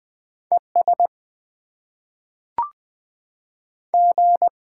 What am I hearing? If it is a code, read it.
Morse code, heard as a keyed tone, about 20 wpm, 700 Hz. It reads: ES G
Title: Top 20 QSO Elements - 6x Character Spacing - 20wpm